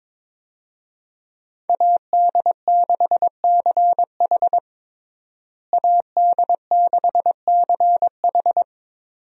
Morse code, heard as a keyed tone, about 22 wpm, 700 Hz. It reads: AD6CH AD6CH
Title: USA Call Signs - 5 Characters - Repeated 2x - 22wpm